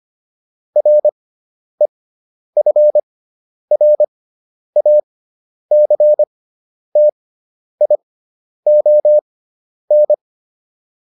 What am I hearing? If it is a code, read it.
Morse code, heard as a keyed tone, about 25 wpm, spaced as for 10 wpm, 600 Hz. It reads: REFRACTION